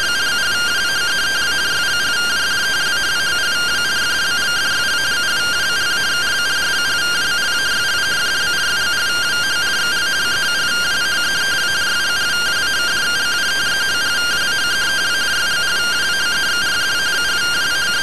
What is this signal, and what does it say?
Two-tone teleprinter signal, about 170 Hz shift.